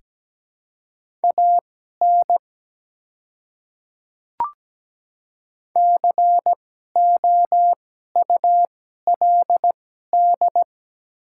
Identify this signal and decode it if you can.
Morse code, heard as a keyed tone, about 17 wpm, spaced as for 12 wpm, 690 Hz. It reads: AN COULD
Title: 2x Character Spacing - Top 300 Words - 17wpm